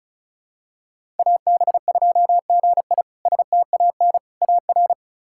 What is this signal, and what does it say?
Morse code, heard as a keyed tone, about 36 wpm, 700 Hz. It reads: AB2GI STAN AR
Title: K1USN SST QSO Practice - 35wpm